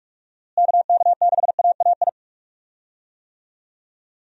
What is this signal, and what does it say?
Morse code, heard as a keyed tone, about 46 wpm, 700 Hz. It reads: KK6AAI
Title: USA Call Signs - 6 Characters - Repeated 3x - 45wpm